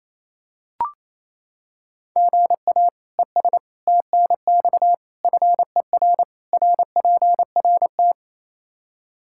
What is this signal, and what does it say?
Morse code, heard as a keyed tone, about 28 wpm, 700 Hz. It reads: GA ES TNX FER RPRT
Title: Practice QSO - Exchange 1 - Beginning - 28wpm